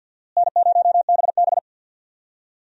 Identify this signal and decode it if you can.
Morse code, heard as a keyed tone, about 51 wpm, 700 Hz. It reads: N0BB
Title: USA Call Signs - 4 Characters - Repeated 3x - 50wpm